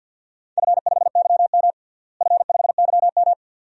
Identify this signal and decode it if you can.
Morse code, heard as a keyed tone, about 53 wpm, 700 Hz. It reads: F5YM F5YM
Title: DXCC-100 Call Signs - 4 Characters - Repeated 2x - 50wpm